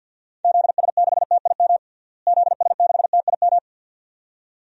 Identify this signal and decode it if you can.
Morse code, heard as a keyed tone, about 51 wpm, 700 Hz. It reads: ZS6TIM ZS6TIM
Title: DXCC-100 Call Signs - 6 Characters - Repeated 2x - 50wpm